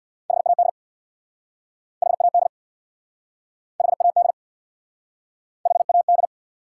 Morse code, heard as a keyed tone, about 69 wpm, 700 Hz. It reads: HAD HAD HAD HAD